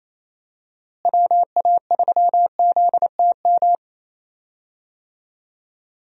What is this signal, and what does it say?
Morse code, heard as a keyed tone, about 28 wpm, 700 Hz. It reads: WA3ZTM
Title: USA Call Signs - 6 Characters - Repeated 2x - 28wpm